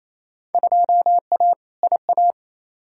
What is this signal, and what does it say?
Morse code, heard as a keyed tone, about 28 wpm, 700 Hz. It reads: 2A IA
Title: Field Day Exchanges Encore - 28wpm